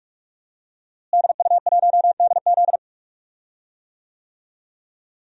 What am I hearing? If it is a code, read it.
Morse code, heard as a keyed tone, about 46 wpm, 700 Hz. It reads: DU1DZ
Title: DXCC-100 Call Signs - 5 Characters - Repeated 3x - 45wpm